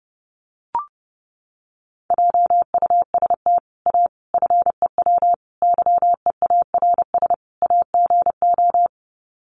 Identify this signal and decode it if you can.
Morse code, heard as a keyed tone, about 30 wpm, 700 Hz. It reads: JUST A FEW YEARS AGO
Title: Sets of 5 Words Encore 30wpm